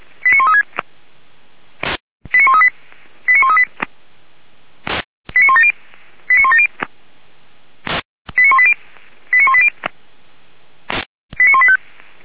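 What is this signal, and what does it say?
Five-tone selective call (ZVEI), heard as quick runs of five tones, tones of about 80 ms. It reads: 89136 89138 89138 89170 89170 89180 89180 89185
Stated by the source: ZVEI five-tone selective call received on an SDR by the Feuerwehr Orlamünde fire brigade.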